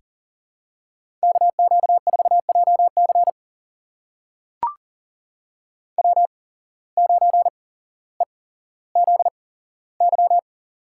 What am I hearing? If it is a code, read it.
Morse code, heard as a keyed tone, about 37 wpm, spaced as for 17 wpm, 700 Hz. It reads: KQ4JC W9EZY
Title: USA Call Signs - 5 Characters - 8x Character Spacing - 40wpm